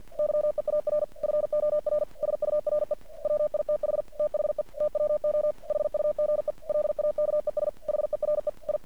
Morse code, heard as a keyed tone, about 50 wpm, 600 Hz. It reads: YEAR FOR SURE WITH THE TWO HUGE LAKES HERE I
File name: archived / CRSnet